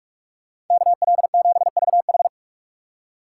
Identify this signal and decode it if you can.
Morse code, heard as a keyed tone, about 42 wpm, 700 Hz. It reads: KL7VH